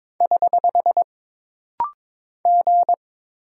Morse code, heard as a keyed tone, about 22 wpm, 700 Hz. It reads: <HH> G
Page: This rapid-fire timing is geared to push your ICR (Instant Character Recognition) skills.